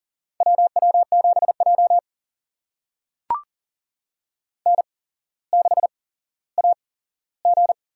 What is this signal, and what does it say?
Morse code, heard as a keyed tone, about 37 wpm, spaced as for 17 wpm, 700 Hz. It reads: WW7J N6AG
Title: USA Call Signs - 4 Characters - 8x Character Spacing - 40wpm